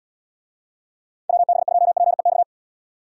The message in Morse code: WB2PEX

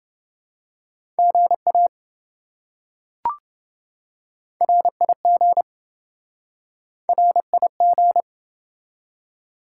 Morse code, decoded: GA RIG RIG

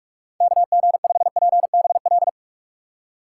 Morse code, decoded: KG5PBL

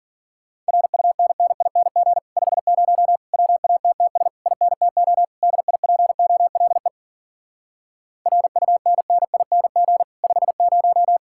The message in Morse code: RUNNING 50 WATTS INTO DIPOLE RUNNING 50

